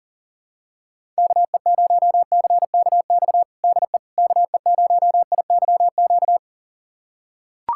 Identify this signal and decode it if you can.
Morse code, heard as a keyed tone, about 37 wpm, 700 Hz. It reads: KE0CKX DE KE0IYQ
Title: USA Callsign DE Callsign - 6 Characters - 40wpm